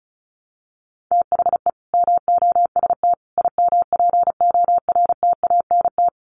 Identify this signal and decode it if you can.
Morse code, heard as a keyed tone, about 34 wpm, 700 Hz. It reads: THE MOST IMPORTANT
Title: Sets of 5 Words Encore 35wpm